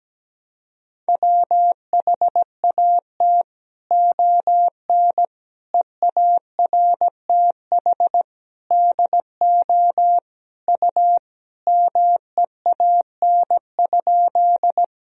Morse code, heard as a keyed tone, about 17 wpm, 700 Hz. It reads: WHAT ON EARTH DO U MEAN?
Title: Sentences from Top 200 Words and Abbreviations 17wpm